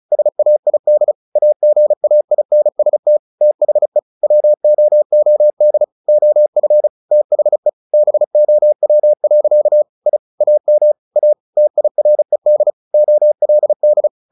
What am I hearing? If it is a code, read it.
Morse code, heard as a keyed tone, about 34 wpm, 600 Hz. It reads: SAID AGAINST THE WOOD OF THE BOW. I AM A TIRED OLD
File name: the old man and the sea 2 35wpm